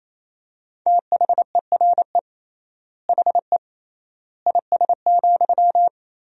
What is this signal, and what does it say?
Morse code, heard as a keyed tone, about 28 wpm, 700 Hz. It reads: THERE HE IS,
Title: 3x Word-Spacing - Sentences from Top 100 Words - 28wpm